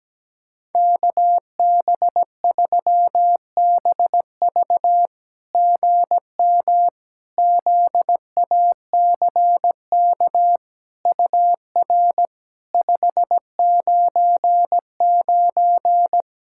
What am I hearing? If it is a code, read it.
Morse code, heard as a keyed tone, about 17 wpm, 700 Hz. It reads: KB3BV GM ZACK UR 599
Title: SOTA QSO Practice 17wpm